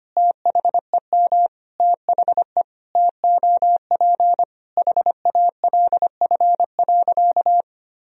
Morse code, heard as a keyed tone, about 25 wpm, 700 Hz. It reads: THEM THE TOP HALF.
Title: Sentences from Top 1000 Words - Review Part 2 - 25wpm